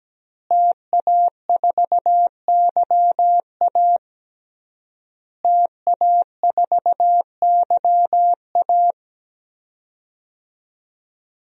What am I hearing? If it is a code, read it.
Morse code, heard as a keyed tone, about 17 wpm, 700 Hz. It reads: TA4YA TA4YA
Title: DXCC-100 Call Signs - 5 Characters - Repeated 3x - 17wpm